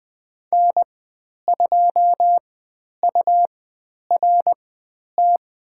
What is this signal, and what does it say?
Morse code, heard as a keyed tone, about 20 wpm, spaced as for 10 wpm, 700 Hz. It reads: N2URT
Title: 102 Lesson 17g - Letter u - Call Signs 20wpm